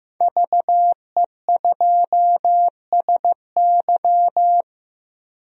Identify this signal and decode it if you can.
Morse code, heard as a keyed tone, about 15 wpm, 700 Hz. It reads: VE2SY
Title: DXCC-100 Call Signs - 5 Characters - 15wpm